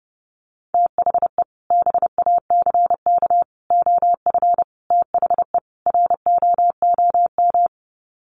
Morse code, heard as a keyed tone, about 30 wpm, 700 Hz. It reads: THE BACK OF THE ROOM
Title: Sets of 5 Words Encore 30wpm